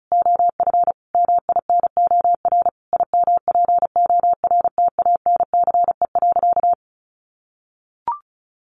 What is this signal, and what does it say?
Morse code, heard as a keyed tone, about 34 wpm, 700 Hz. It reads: OF MINOR IMPORTANCE.
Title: Sentences from Top 1500 Words 35wpm